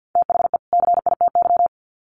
Morse code, heard as a keyed tone, about 50 wpm, 700 Hz. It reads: THE CITY
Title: Sets of 2 Words 50wpm